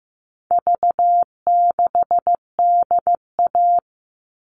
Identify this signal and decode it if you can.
Morse code, heard as a keyed tone, about 15 wpm, 700 Hz. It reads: V6DA
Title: All Cut Numbers - 4-Digit 15wpm